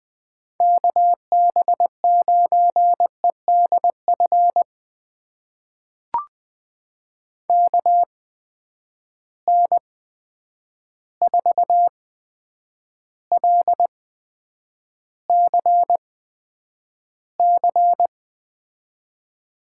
Morse code, heard as a keyed tone, about 20 wpm, spaced as for 9 wpm, 700 Hz. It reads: KB9EDF KN4LCC
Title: USA Call Signs - 6 Characters - 8x Character Spacing - 20wpm